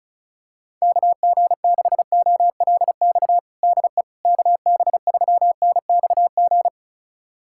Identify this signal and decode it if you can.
Morse code, heard as a keyed tone, about 36 wpm, 700 Hz. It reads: KG6OLX DE KB3NXG
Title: USA Callsign DE Callsign - 6 Characters - 35wpm